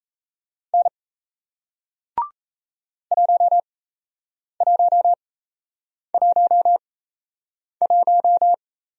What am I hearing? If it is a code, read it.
Morse code, heard as a keyed tone, about 37 wpm, spaced as for 8 wpm, 700 Hz. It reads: N 1111